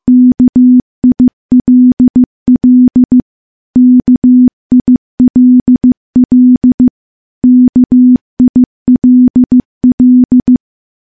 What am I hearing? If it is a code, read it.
Morse code, heard as a keyed tone, about 15 wpm, 250 Hz. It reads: KILL KILL KILL
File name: morse / morse-training-15WPM